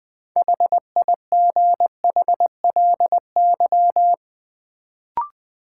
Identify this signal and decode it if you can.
Morse code, heard as a keyed tone, about 20 wpm, 700 Hz. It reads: HIGHLY
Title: Sentences from Top 1100 Words - New Word Drill Alt - 20wpm